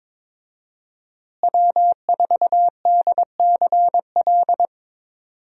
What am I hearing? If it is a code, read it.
Morse code, heard as a keyed tone, about 22 wpm, 700 Hz. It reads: W4DCL
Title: USA Call Signs - 5 Characters - Repeated 2x - 22wpm